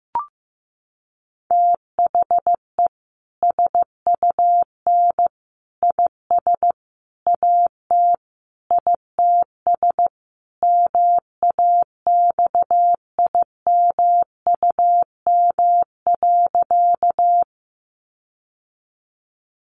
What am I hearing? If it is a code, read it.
Morse code, heard as a keyed tone, about 15 wpm, 700 Hz. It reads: THE SUN IS AT ITS MAXIMUM.
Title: Sentences from Top 1300 Words 15wpm